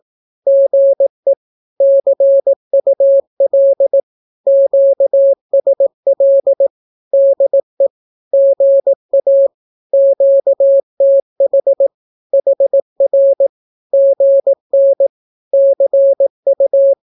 Morse code, heard as a keyed tone, about 18 wpm, 550 Hz. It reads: GE CUL QSL DE GA QTH HR GN CU
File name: naukapodstaw / skroty / skroty